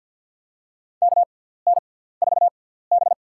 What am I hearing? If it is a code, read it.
Morse code, heard as a keyed tone, about 50 wpm, spaced as for 17 wpm, 700 Hz. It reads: KN4B
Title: USA Call Signs - 4 Characters - 6x Character Spacing - 50wpm